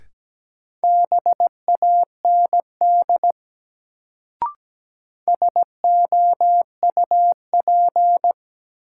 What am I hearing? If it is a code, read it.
Morse code, heard as a keyed tone, about 17 wpm, 700 Hz. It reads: BAND SOUP
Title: Sentences from Top 1600 Words - New Word Drill Alt - 17wpm